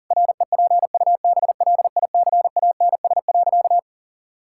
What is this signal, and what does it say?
Morse code, heard as a keyed tone, about 40 wpm, 700 Hz. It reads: REPUBLICANS.
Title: Pangram Challenge v2 - 40wpm